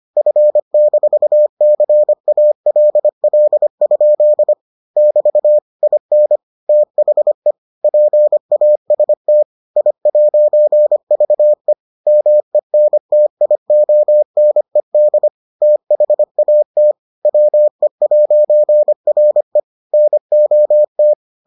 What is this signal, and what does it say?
Morse code, heard as a keyed tone, about 25 wpm, 600 Hz. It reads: F-CALL? <BT> IN THE PAST I'VE MENTIONED THAT WE'RE NOT